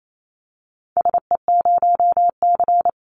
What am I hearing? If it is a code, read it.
Morse code, heard as a keyed tone, about 28 wpm, 700 Hz. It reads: SE0C
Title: Call Signs Repeated 3x - 28wpm